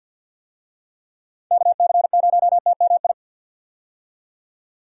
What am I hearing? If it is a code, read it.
Morse code, heard as a keyed tone, about 50 wpm, 700 Hz. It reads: KC0TMI
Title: USA Call Signs - 6 Characters - 50wpm